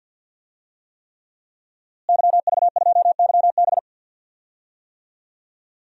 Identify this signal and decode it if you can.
Morse code, heard as a keyed tone, about 46 wpm, 700 Hz. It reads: YV2YB
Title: DXCC-100 Call Signs - 5 Characters - Repeated 3x - 50wpm